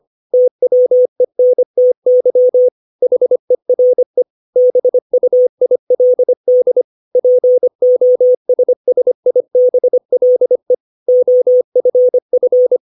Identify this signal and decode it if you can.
Morse code, heard as a keyed tone, about 25 wpm, 500 Hz. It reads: TWENTY HERE BUILD POSSIBLE OFF
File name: 632 words / 632 Words 25WPM